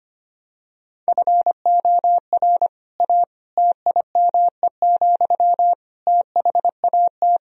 Morse code, heard as a keyed tone, about 25 wpm, 700 Hz. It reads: FOR A TIME, THAT
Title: Sentences from Top 1000 Words - Review Part 4 - 25wpm